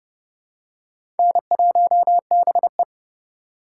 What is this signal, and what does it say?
Morse code, heard as a keyed tone, about 30 wpm, 700 Hz. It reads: N1BE
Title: USA Call Signs - 4 Characters - 2x Character Spacing - 30wpm